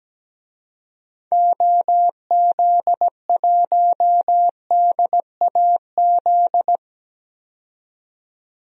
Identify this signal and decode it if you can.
Morse code, heard as a keyed tone, about 17 wpm, 700 Hz. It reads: OZ1DAZ
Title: DXCC-100 Call Signs - 6 Characters - 17wpm